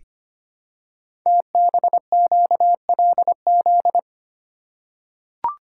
Morse code, heard as a keyed tone, about 25 wpm, 700 Hz. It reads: TBQLZ
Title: Code Groups - 5 Letter:Numbers - 25wpm